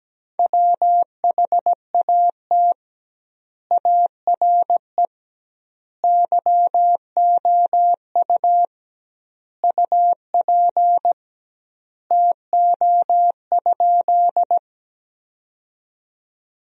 Morse code, heard as a keyed tone, about 17 wpm, 700 Hz. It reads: WHAT ARE YOU UP TO?
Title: 2x Word-Spacing - Sentences from Top 100 Words - 17wpm